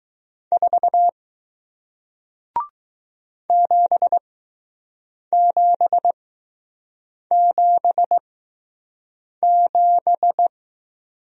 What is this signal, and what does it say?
Morse code, heard as a keyed tone, about 20 wpm, spaced as for 6 wpm, 700 Hz. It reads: 4 7777